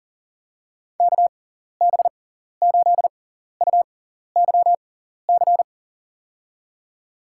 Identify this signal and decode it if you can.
Morse code, heard as a keyed tone, about 40 wpm, spaced as for 14 wpm, 700 Hz. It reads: KB8UYC
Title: USA Call Signs - 6 Characters - 6x Character Spacing - 40wpm